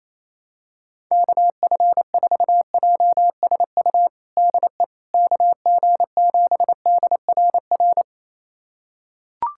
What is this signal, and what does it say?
Morse code, heard as a keyed tone, about 27 wpm, 700 Hz. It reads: KF4JSU DE KG7DRR
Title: USA Callsign DE Callsign 28wpm